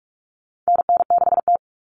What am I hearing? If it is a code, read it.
Morse code, heard as a keyed tone, about 44 wpm, 700 Hz. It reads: NN6T
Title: Call Signs - 45wpm